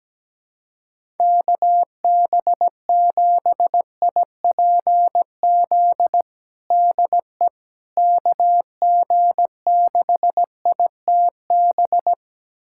Morse code, heard as a keyed tone, about 17 wpm, 700 Hz. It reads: KB7IPZ DE KG6ITB
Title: USA Callsign DE Callsign 17wpm